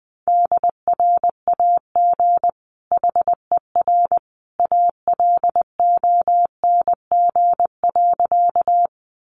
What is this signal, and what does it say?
Morse code, heard as a keyed tone, about 20 wpm, 700 Hz. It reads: DRAG HER ALONG.
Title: Sentences from Top 1500 Words 20wpm